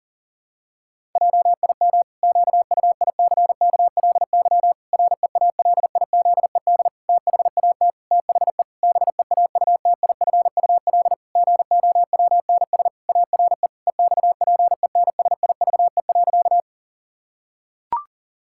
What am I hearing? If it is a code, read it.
Morse code, heard as a keyed tone, about 40 wpm, 700 Hz. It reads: JIM QUICKLY REALIZED THAT THE BEAUTIFUL GOWNS ARE EXPENSIVE.
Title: Pangram Challenge v2 - 40wpm